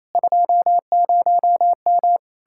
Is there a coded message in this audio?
Yes: Morse code from 0.2 to 2.2 s, heard as a keyed tone, about 37 wpm, 700 Hz. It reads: 20M